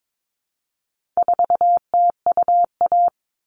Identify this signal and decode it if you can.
Morse code, heard as a keyed tone, about 22 wpm, 700 Hz. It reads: 4TUA
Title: All Cut Numbers - 4-Digit 22wpm